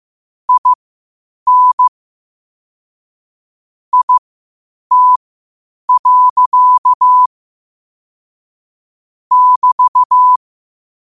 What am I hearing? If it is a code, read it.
Morse code, heard as a keyed tone, about 15 wpm, spaced as for 8 wpm, 1000 Hz. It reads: IN IT. <BT>
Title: 8 WPM morse code quotes for Thu, 29 Feb 2024